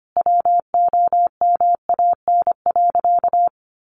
Morse code, heard as a keyed tone, about 25 wpm, 700 Hz. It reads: WOMAN.
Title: Sentences from Top 1400 Words 25wpm